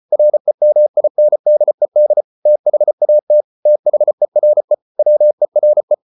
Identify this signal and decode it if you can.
Morse code, heard as a keyed tone, about 34 wpm, 600 Hz. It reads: REMINDED THAT THERE WERE